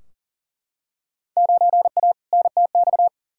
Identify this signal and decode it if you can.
Morse code, heard as a keyed tone, about 40 wpm, 700 Hz. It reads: 9A NTX
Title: Field Day Exchanges Encore - 40wpm